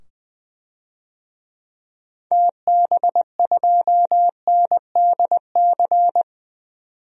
Morse code, heard as a keyed tone, about 20 wpm, 700 Hz. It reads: TB2NDC
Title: DXCC-100 Call Signs - 6 Characters - 20wpm